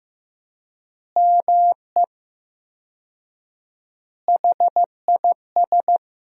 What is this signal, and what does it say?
Morse code, heard as a keyed tone, about 15 wpm, 700 Hz. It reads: ME HIS
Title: 4x Word-Spacing - Sentences from Top 200 Words - 15wpm